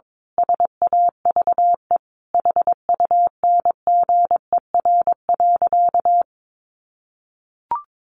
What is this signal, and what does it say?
Morse code, heard as a keyed tone, about 22 wpm, 700 Hz. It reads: SAVE HUNGER.
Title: A Sound Of Thunder 22wpm